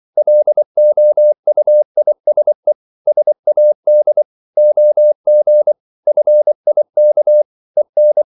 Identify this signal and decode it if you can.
Morse code, heard as a keyed tone, about 24 wpm, 600 Hz. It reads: LOUISE SAD OG FIK EN
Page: Ferie 24 wpm